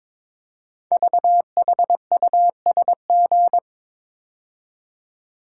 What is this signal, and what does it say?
Morse code, heard as a keyed tone, about 22 wpm, 700 Hz. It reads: VHUSG
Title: Code Groups - 5 Letters - 22wpm